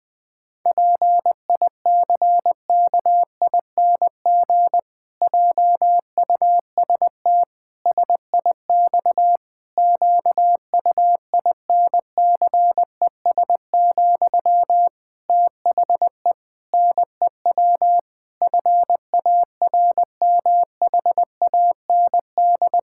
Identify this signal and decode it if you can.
Morse code, heard as a keyed tone, about 21 wpm, 700 Hz. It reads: PICKING JUST SIX QUINCES, THE NEW FARMHAND
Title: Pangram Challenge v2 - 20wpm